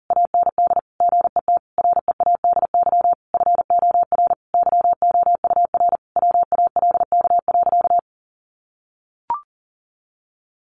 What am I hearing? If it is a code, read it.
Morse code, heard as a keyed tone, about 40 wpm, 700 Hz. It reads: AND GET READY FOR YOUR WALK.
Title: Sentences from Top 800 Words 40wpm